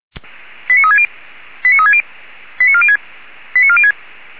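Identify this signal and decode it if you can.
Five-tone selective call (ZVEI), heard as quick runs of five tones, tones of about 70 ms. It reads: 89360 89360 89406 89406
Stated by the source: ZVEI five-tone selective call 89288, received on an SDR, from the feuerwehr-orlamuende fire brigade audio server.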